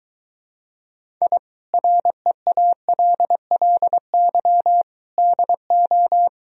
Morse code, heard as a keyed tone, about 24 wpm, 700 Hz. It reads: I REALLY DO